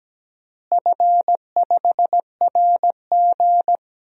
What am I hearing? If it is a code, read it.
Morse code, heard as a keyed tone, about 17 wpm, 700 Hz. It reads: F5RG
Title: DXCC-100 Call Signs - 4 Characters - Repeated 3x - 17wpm